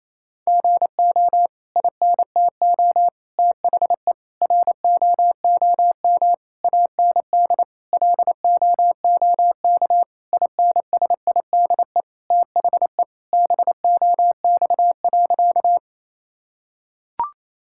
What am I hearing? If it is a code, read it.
Morse code, heard as a keyed tone, about 28 wpm, 700 Hz. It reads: GO INTO THE ROOM AND LOOK INSIDE THE BOX.
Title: Sentences from Top 1000 Words - Review Part 6 - 28wpm